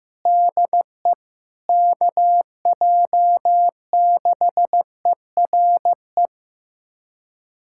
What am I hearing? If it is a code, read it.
Morse code, heard as a keyed tone, about 15 wpm, 700 Hz. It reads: DE KJ6ERE
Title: USA Callsign DE Callsign - 6 Characters - 15wpm